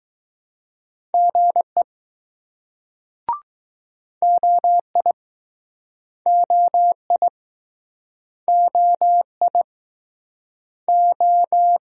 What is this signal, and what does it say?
Morse code, heard as a keyed tone, about 20 wpm, 700 Hz. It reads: GE OI OI OI O